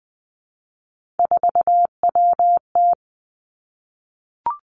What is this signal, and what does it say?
Morse code, heard as a keyed tone, about 20 wpm, 700 Hz. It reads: 4WT
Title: Three Challenging Letter-Number Combinations 20wpm